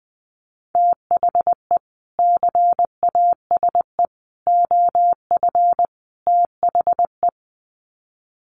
Morse code, heard as a keyed tone, about 20 wpm, 700 Hz. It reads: THE CASE OF THE
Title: Sets of 4 Words 20wpm